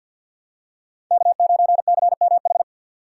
Morse code, heard as a keyed tone, about 49 wpm, 700 Hz. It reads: K9CMH
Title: USA Call Signs - 2x Character Spacing - 50wpm